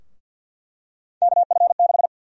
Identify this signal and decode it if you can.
Morse code, heard as a keyed tone, about 48 wpm, 700 Hz. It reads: KF6
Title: USA Call Sign Prefixes - 50wpm